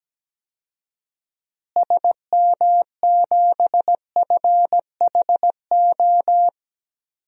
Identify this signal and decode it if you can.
Morse code, heard as a keyed tone, about 17 wpm, 700 Hz. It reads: SM7FHO
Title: DXCC-100 Call Signs - 6 Characters - Repeated 3x - 17wpm